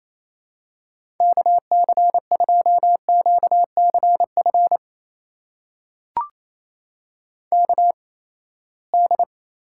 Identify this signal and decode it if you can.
Morse code, heard as a keyed tone, about 28 wpm, spaced as for 19 wpm, 700 Hz. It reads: KC2QCF KD
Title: USA Call Signs - 6 Characters - 8x Character Spacing - 28wpm